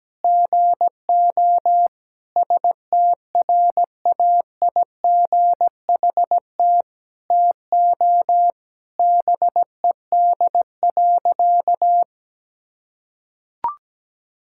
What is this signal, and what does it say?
Morse code, heard as a keyed tone, about 17 wpm, 700 Hz. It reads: GO STRAIGHT TO BED.
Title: Sentences from Top 1000 Words - Review Part 6 - 17wpm